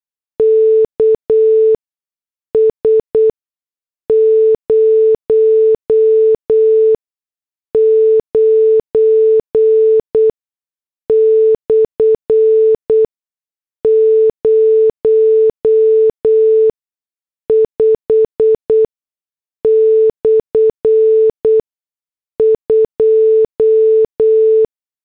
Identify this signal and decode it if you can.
Morse code, heard as a keyed tone, about 8 wpm, spaced as for 6 wpm, 440 Hz. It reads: KS 09/05/2
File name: morse / headlines